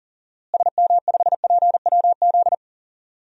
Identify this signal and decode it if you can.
Morse code, heard as a keyed tone, about 40 wpm, 700 Hz. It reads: SM5PWZ
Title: DXCC-100 Call Signs - 6 Characters - Repeated 3x - 40wpm